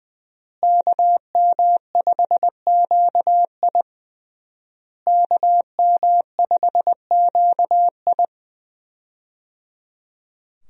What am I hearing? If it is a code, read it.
Morse code, heard as a keyed tone, about 20 wpm, 700 Hz. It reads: KM5QI KM5QI
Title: USA Call Signs - 5 Characters - Repeated 3x - 20wpm